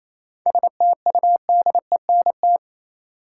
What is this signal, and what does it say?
Morse code, heard as a keyed tone, about 28 wpm, 700 Hz. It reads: STUDENT
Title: Sentences from Top 1000 Words - New Word Drill - 28wpm